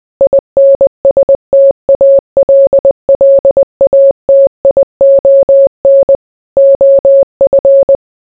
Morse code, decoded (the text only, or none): INSTALLATION OF